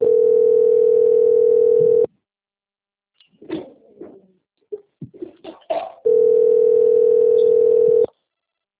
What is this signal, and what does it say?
Ringback tone (North America), 440 + 480 Hz.